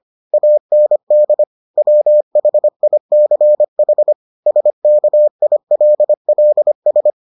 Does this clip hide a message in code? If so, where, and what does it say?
Yes: Morse code from 0.3 to 7.1 s, heard as a keyed tone, about 25 wpm, 600 Hz. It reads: AND WHICH SKILLS